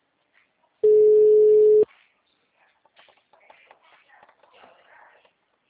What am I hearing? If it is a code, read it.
Ringback tone (Europe, 425 Hz).
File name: out- -2011- - - .2710